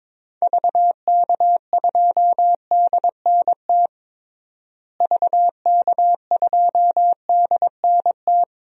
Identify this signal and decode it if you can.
Morse code, heard as a keyed tone, about 22 wpm, 700 Hz. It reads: VK2DNT VK2DNT
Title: DXCC-100 Call Signs - 6 Characters - Repeated 3x - 22wpm